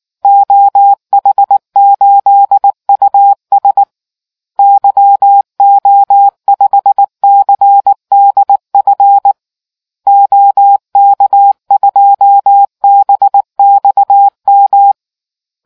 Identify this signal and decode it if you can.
Morse code, heard as a keyed tone, about 19 wpm, 800 Hz. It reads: OH8US YO5CDF OK2BXM